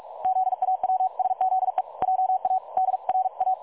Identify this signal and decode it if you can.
Morse code, heard as a keyed tone, about 44 wpm, 750 Hz. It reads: ZAWSZE JA DRA
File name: dlazaawansowanych / nagraniazpasma / websdr recording 3565.0kHz